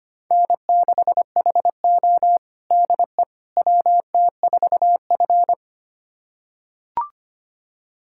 Morse code, heard as a keyed tone, about 25 wpm, 700 Hz. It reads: N6HO DE WT4F
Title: USA Callsign DE Callsign - 4 Characters - 25wpm